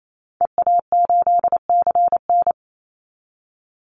Morse code, heard as a keyed tone, about 28 wpm, 700 Hz. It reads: EA8CN
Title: Call Signs Repeated 3x - 28wpm